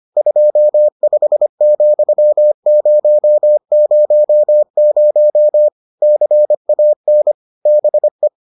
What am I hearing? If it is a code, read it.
Morse code, heard as a keyed tone, about 25 wpm, 600 Hz. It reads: 25,000 CAN BE